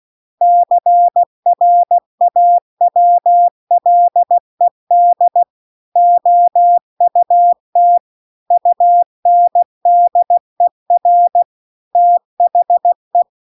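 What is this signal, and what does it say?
Morse code, heard as a keyed tone, about 16 wpm, 700 Hz. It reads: CRAWLED OUT UNDER THE
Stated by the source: Morse practice text at 16 WPM.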